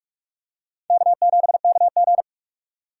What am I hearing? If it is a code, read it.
Morse code, heard as a keyed tone, about 45 wpm, 700 Hz. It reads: K7KG
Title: USA Call Signs - 4 Characters - 2x Character Spacing - 45wpm